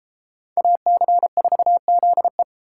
Morse code, heard as a keyed tone, about 33 wpm, 700 Hz. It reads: AC4ZE